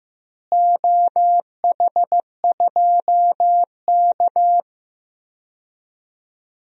Morse code, heard as a keyed tone, about 15 wpm, 700 Hz. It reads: OH2K
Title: DXCC-100 Call Signs - 4 Characters - Repeated 2x - 15wpm